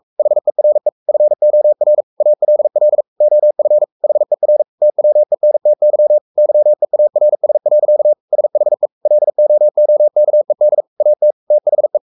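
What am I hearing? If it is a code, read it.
Morse code, heard as a keyed tone, about 44 wpm, 600 Hz. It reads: HERE FOR ALL OF HER TWENTY YEARS. SHE LOOKED AT THE
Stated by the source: Morse code at 43 wpm sending Ken Follett's novel Never in English, from CW med Gnister.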